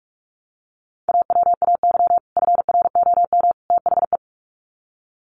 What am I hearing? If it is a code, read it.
Morse code, heard as a keyed tone, about 44 wpm, 700 Hz. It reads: AWAY FROM THE
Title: Sets of 3 Words 45wpm